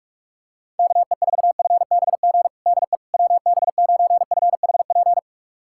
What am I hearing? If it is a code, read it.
Morse code, heard as a keyed tone, about 45 wpm, 700 Hz. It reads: KE4FBG DE WB9FHP
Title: USA Callsign DE Callsign - 6 Characters - 45wpm